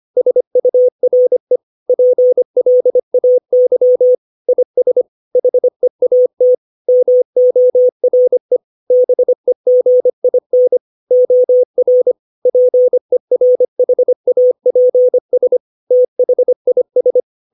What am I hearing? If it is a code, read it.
Morse code, heard as a keyed tone, about 25 wpm, 500 Hz. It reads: SURE PLAY IS HEAT MORE BEGIN OR PERHAPS THIS
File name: 632 words / 632 Words 25WPM